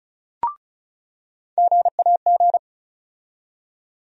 Morse code, heard as a keyed tone, about 36 wpm, 700 Hz. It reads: GAG
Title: USA Call Sign Postfixes - 3 Letters 35wpm